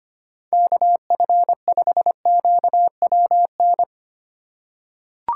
Morse code, heard as a keyed tone, about 25 wpm, 700 Hz. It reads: KF5QWN
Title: USA Call Signs - 6 Characters - 4x Character Spacing - 25wpm